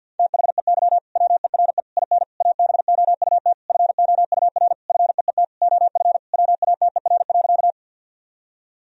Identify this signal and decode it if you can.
Morse code, heard as a keyed tone, about 52 wpm, 700 Hz. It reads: THEY WERE IN ABOUT FOUR FEET OF WATER.
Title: Sentences from Top 1000 Words - Review Part 1 - 50wpm